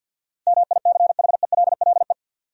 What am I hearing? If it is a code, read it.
Morse code, heard as a keyed tone, about 48 wpm, 700 Hz. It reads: MICHELLE
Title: USA Common Names - 50wpm